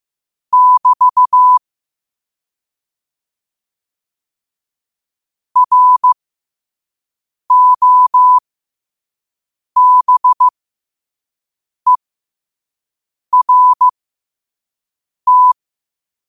Morse code, heard as a keyed tone, about 15 wpm, spaced as for 5 wpm, 1000 Hz. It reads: <BT> ROBERT
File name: qotdfiles / 5wpm